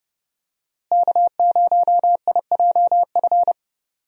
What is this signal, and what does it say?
Morse code, heard as a keyed tone, about 30 wpm, 700 Hz. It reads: K0IJF
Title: USA Call Signs - 5 Characters - Repeated 3x - 30wpm